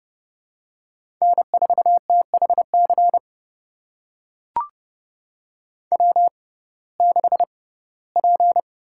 Morse code, heard as a keyed tone, about 30 wpm, spaced as for 18 wpm, 700 Hz. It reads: N4THC W6P
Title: USA Call Signs - 5 Characters - 6x Character Spacing - 30wpm